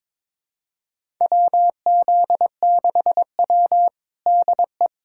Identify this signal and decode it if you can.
Morse code, heard as a keyed tone, about 22 wpm, 700 Hz. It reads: WZ6W DE
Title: USA Callsign DE Callsign - 4 Characters - 22wpm